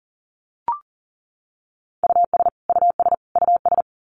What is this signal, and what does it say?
Morse code, heard as a keyed tone, about 40 wpm, 700 Hz. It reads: US US US
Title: Top 500 Words Repeated 40wpm